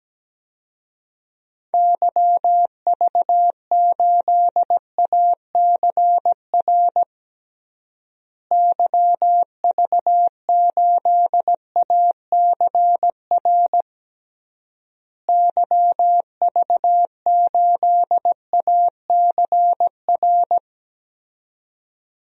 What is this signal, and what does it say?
Morse code, heard as a keyed tone, about 17 wpm, 700 Hz. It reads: YV8ACR YV8ACR YV8ACR
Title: DXCC-100 Call Signs - 6 Characters - Repeated 3x - 17wpm